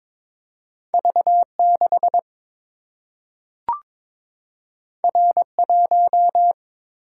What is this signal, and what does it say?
Morse code, heard as a keyed tone, about 22 wpm, 700 Hz. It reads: V6 R1
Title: Single-Letter Single-Number 22wpm